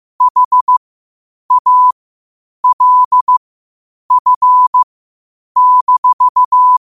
Morse code, heard as a keyed tone, about 15 wpm, spaced as for 8 wpm, 1000 Hz. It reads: HALF-
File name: qotdfiles / 8wpm